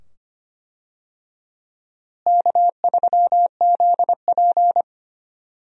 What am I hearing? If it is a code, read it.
Morse code, heard as a keyed tone, about 25 wpm, 700 Hz. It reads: K3ZP
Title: USA Call Signs - 4 Characters - Repeated 2x - 25wpm